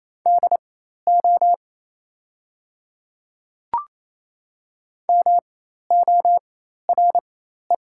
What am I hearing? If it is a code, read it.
Morse code, heard as a keyed tone, about 28 wpm, spaced as for 13 wpm, 700 Hz. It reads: DO MORE
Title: 4x Character Spacing - Top 100 Words - 28wpm